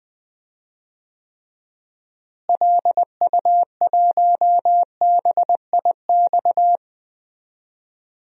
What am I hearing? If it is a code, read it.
Morse code, heard as a keyed tone, about 20 wpm, 700 Hz. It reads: LU1BIX
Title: DXCC-100 Call Signs - 6 Characters - Repeated 3x - 20wpm